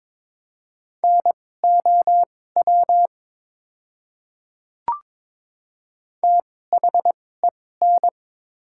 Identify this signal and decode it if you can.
Morse code, heard as a keyed tone, about 23 wpm, spaced as for 16 wpm, 690 Hz. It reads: NOW THEN
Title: 2x Character Spacing - Top 200 Words - 22wpm